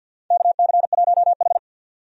Morse code, heard as a keyed tone, about 49 wpm, 700 Hz. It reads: KC1H
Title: USA Call Signs - 4 Characters - 50wpm